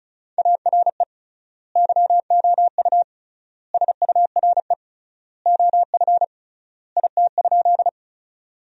Morse code, heard as a keyed tone, about 36 wpm, 700 Hz. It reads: ARE YOU SURE OF IT?
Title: 3x Word-Spacing - Sentences from Top 300 Words - 35wpm